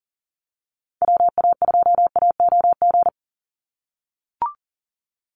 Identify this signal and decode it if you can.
Morse code, heard as a keyed tone, about 40 wpm, 700 Hz. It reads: WA2AOG
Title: Call Signs - 40wpm